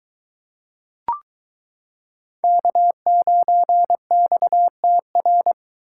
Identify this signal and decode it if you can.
Morse code, heard as a keyed tone, about 22 wpm, 700 Hz. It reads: K9XTR